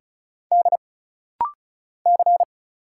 Morse code, heard as a keyed tone, about 34 wpm, 700 Hz. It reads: D C
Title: Single Letters - Rapid-Fire 35wpm